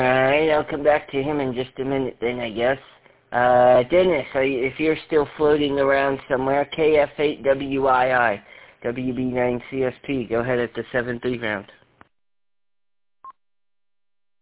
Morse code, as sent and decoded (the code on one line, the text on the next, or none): none